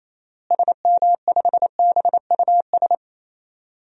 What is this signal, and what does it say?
Morse code, heard as a keyed tone, about 28 wpm, 700 Hz. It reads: SM5BUS
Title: DXCC-100 Call Signs - 6 Characters - Repeated 2x - 28wpm